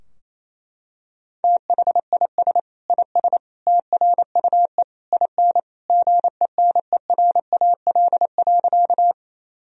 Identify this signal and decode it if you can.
Morse code, heard as a keyed tone, about 28 wpm, 700 Hz. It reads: THIS IS TRUE IN GENERAL.
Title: Sentences from Top 1000 Words - Review Part 3 - 28wpm